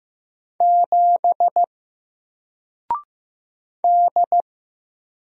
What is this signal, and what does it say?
Morse code, heard as a keyed tone, about 15 wpm, 700 Hz. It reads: 7 D
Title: Single Letter-Number Alt 15wpm